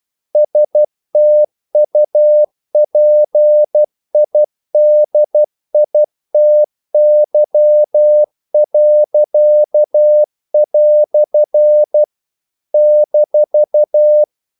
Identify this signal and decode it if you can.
Morse code, heard as a keyed tone, about 12 wpm, 600 Hz. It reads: STUPIDITY." -
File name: CW / Practice / EnglishQuotations0112